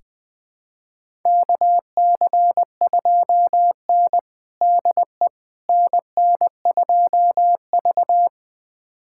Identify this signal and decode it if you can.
Morse code, heard as a keyed tone, about 21 wpm, 700 Hz. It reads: KC2N DE NN2V
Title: USA Callsign DE Callsign - 4 Characters - 20wpm